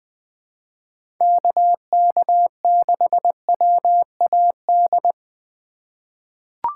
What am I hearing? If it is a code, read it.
Morse code, heard as a keyed tone, about 20 wpm, 700 Hz. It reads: KK6WAD
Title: USA Call Signs - 6 Characters - 3x Character Spacing - 20wpm